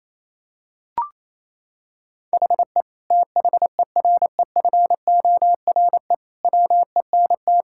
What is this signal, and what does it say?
Morse code, heard as a keyed tone, about 28 wpm, 700 Hz. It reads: HE THEREFORE WENT
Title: Sentences from Top 1000 Words - Review Part 5 - 28wpm